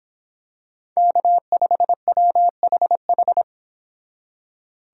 Morse code, heard as a keyed tone, about 26 wpm, 700 Hz. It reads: K5WHH